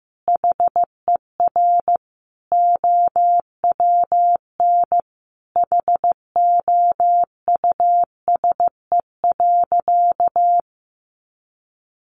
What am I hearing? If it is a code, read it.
Morse code, heard as a keyed tone, about 15 wpm, 700 Hz. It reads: HER OWN HOUSE.
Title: Sentences from Top 300 Words 15wpm